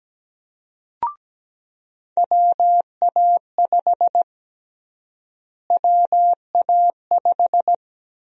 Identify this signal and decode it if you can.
Morse code, heard as a keyed tone, about 17 wpm, 700 Hz. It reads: WA5 WA5
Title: USA Call Sign Prefixes - Repeated 2x - 17wpm